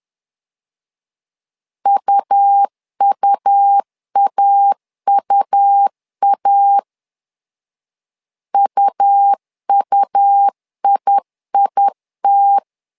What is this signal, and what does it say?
Morse code, heard as a keyed tone, about 10 wpm, 800 Hz. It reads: UUAUA UUIIT
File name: LEZIONI / 2 UHDN / 2 U 10wpm 1622